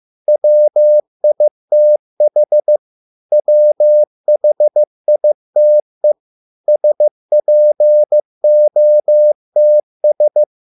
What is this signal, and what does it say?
Morse code, heard as a keyed tone, about 15 wpm, 600 Hz. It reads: WITH WHITE SPOTS